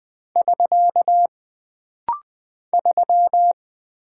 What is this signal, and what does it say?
Morse code, heard as a keyed tone, about 20 wpm, 700 Hz. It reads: <SK> 3
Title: Single Character (Letter:Number:Punctuation:Prosign) - Rapid-Fire - 20wpm